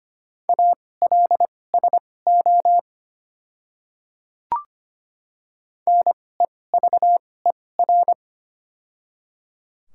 Morse code, heard as a keyed tone, about 25 wpm, spaced as for 18 wpm, 700 Hz. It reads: ALSO NEVER
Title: 2x Character Spacing - Top 500 Words - 25wpm